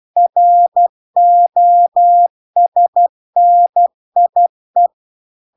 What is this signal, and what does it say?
Morse code, heard as a keyed tone, about 12 wpm, 700 Hz. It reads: ROSNIE